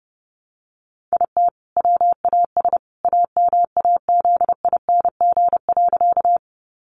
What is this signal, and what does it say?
Morse code, heard as a keyed tone, about 30 wpm, 700 Hz. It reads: IT WAS AMAZING.
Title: Sentences from Top 1500 Words 30wpm